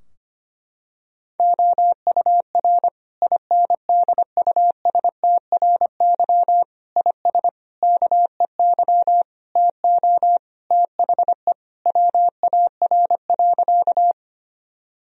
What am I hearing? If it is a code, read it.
Morse code, heard as a keyed tone, about 25 wpm, 700 Hz. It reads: OUR INDUSTRY IS KEY TO THE WAR.
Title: Sentences from Top 1000 Words - Review Part 4 - 25wpm